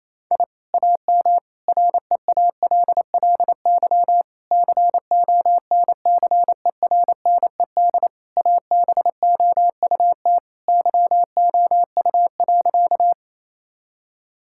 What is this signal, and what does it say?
Morse code, heard as a keyed tone, about 28 wpm, 700 Hz. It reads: I AM REALLY CONCERNED ABOUT YOU.
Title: Sentences from Top 1000 Words - Review Part 6 - 28wpm